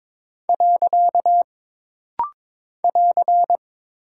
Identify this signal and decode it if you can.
Morse code, heard as a keyed tone, about 22 wpm, 700 Hz. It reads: . <AR>
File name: practice / 22wpm / Single Punctuation-Prosign - Rapid-Fire - 22wpm